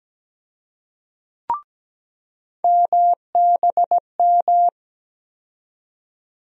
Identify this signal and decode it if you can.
Morse code, heard as a keyed tone, about 17 wpm, 700 Hz. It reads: MBM
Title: USA Call Sign Postfixes - 3 Letters 17wpm